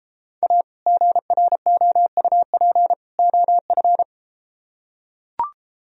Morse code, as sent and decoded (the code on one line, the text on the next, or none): .- / --. .-. --- ..- .--. / --- ..-.
A GROUP OF